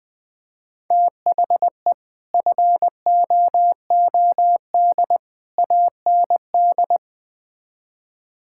Morse code, heard as a keyed tone, about 20 wpm, 700 Hz. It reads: THE FOOD AND
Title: Build A Phrase - 5-Word Phrases - 20wpm